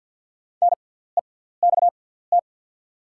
Morse code, heard as a keyed tone, about 48 wpm, spaced as for 17 wpm, 700 Hz. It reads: NEXT
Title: 6x Character Spacing - Top 300 Words - 50wpm